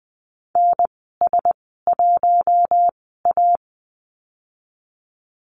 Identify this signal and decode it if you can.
Morse code, heard as a keyed tone, about 20 wpm, spaced as for 15 wpm, 700 Hz. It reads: NS1A
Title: 042 Lesson 8g - Number 1 - Call Signs 20wpm